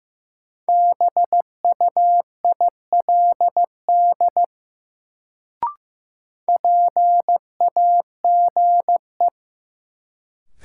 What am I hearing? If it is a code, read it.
Morse code, heard as a keyed tone, about 15 wpm, 700 Hz. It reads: BUILD PAGE
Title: Sentences from Top 1000 Words - Review Part 1 - New Word Drill - 15wpm